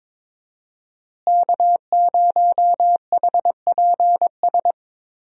Morse code, heard as a keyed tone, about 22 wpm, 700 Hz. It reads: K0HPS
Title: USA Call Signs - 5 Characters - Repeated 3x - 22wpm